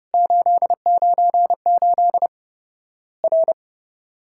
Morse code, heard as a keyed tone, about 30 wpm, 700 Hz. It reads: 898 R
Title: Worked All Europe DX Contest - QTC Exchange - 30wpm